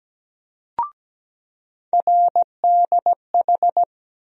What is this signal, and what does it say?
Morse code, heard as a keyed tone, about 17 wpm, 700 Hz. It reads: RDH
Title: USA Call Sign Postfixes - 3 Letters 17wpm